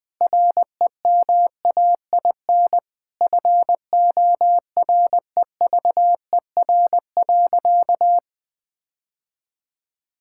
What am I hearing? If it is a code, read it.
Morse code, heard as a keyed tone, about 20 wpm, 700 Hz. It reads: REMAIN FOREVER.
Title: Sentences from Top 1600 Words 20wpm